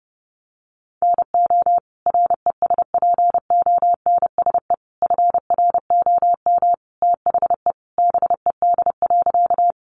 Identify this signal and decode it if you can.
Morse code, heard as a keyed tone, about 30 wpm, 700 Hz. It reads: NO RESPONSE FROM THE BED.
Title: Sentences from Top 1400 Words 30wpm